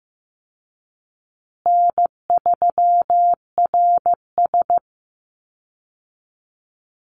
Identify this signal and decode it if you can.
Morse code, heard as a keyed tone, about 15 wpm, 700 Hz. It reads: N3RS
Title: Easy Call Signs Repeated 2x - 15wpm